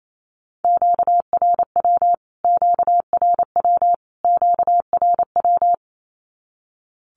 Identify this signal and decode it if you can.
Morse code, heard as a keyed tone, about 28 wpm, 700 Hz. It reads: QRW QRW QRW
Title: Q-Codes Repeated 3x 28wpm